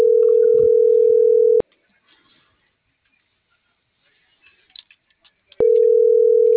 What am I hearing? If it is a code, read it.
Ringback tone (North America), 440 + 480 Hz.